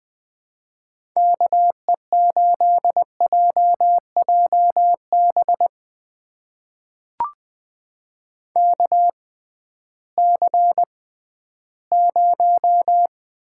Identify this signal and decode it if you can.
Morse code, heard as a keyed tone, about 21 wpm, spaced as for 13 wpm, 700 Hz. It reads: KE8JJB KC0
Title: USA Call Signs - 6 Characters - 6x Character Spacing - 20wpm